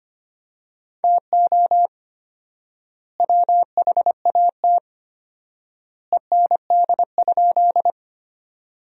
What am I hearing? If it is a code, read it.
Morse code, heard as a keyed tone, about 25 wpm, 700 Hz. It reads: TO WHAT END?
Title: 4x Word-Spacing - Sentences from Top 200 Words - 25wpm